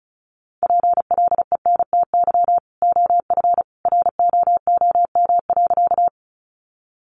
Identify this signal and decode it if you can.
Morse code, heard as a keyed tone, about 34 wpm, 700 Hz. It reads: PLENTY OF ROOM.
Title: Sentences from Top 1300 Words 35wpm